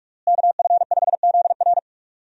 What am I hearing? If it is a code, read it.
Morse code, heard as a keyed tone, about 46 wpm, 700 Hz. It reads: KF5ZR
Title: USA Call Signs - 5 Characters - Repeated 2x - 45wpm